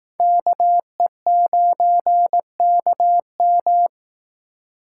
Morse code, heard as a keyed tone, about 19 wpm, 700 Hz. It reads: KE9KM
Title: Speed Racing - USA 2x2 Call Signs 15wpm